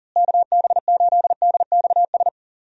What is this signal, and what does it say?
Morse code, heard as a keyed tone, about 40 wpm, 700 Hz. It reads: KB8DXS
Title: USA Call Signs - 6 Characters - Repeated 2x - 40wpm